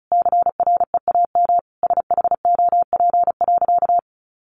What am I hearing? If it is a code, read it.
Morse code, heard as a keyed tone, about 34 wpm, 700 Hz. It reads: CREAM SHOP.
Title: Easy Phrases 35wpm